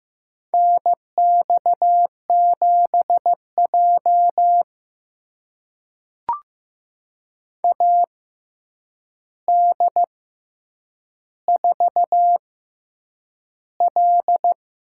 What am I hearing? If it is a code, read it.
Morse code, heard as a keyed tone, about 15 wpm, spaced as for 8 wpm, 700 Hz. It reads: NX7J AD4L
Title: USA Call Signs - 4 Characters - 6x Character Spacing - 15wpm